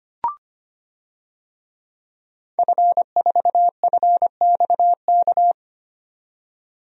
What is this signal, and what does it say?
Morse code, heard as a keyed tone, about 25 wpm, 700 Hz. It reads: F4FXK
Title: DXCC-100 Call Signs - 5 Characters - 25wpm